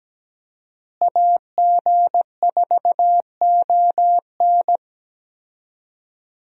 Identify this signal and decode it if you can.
Morse code, heard as a keyed tone, about 17 wpm, 700 Hz. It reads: AG4ON
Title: USA Call Signs - 5 Characters - Repeated 2x - 17wpm